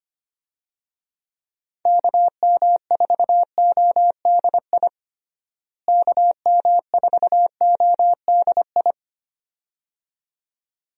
Morse code, heard as a keyed tone, about 25 wpm, 700 Hz. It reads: KM4ODI KM4ODI
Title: USA Call Signs - 6 Characters - Repeated 2x - 25wpm